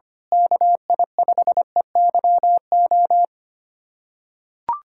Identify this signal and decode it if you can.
Morse code, heard as a keyed tone, about 25 wpm, 700 Hz. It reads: KI5EYO
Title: USA Callsign DE Callsign - 6 Characters - 25wpm